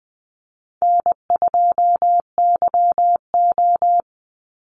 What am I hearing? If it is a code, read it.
Morse code, heard as a keyed tone, about 20 wpm, 700 Hz. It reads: N2YO
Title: Easy Call Signs Repeated 2x - 20wpm